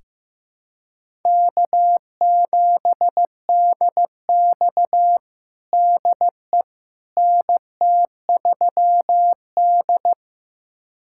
Morse code, heard as a keyed tone, about 16 wpm, 700 Hz. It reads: K7DX DE NT3D
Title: USA Callsign DE Callsign - 4 Characters - 15wpm